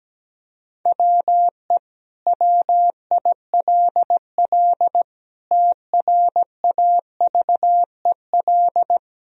Morse code, heard as a keyed tone, about 17 wpm, 700 Hz. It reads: WE WILL TRAVEL
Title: Sentences from Top 1000 Words - Review Part 2 - 17wpm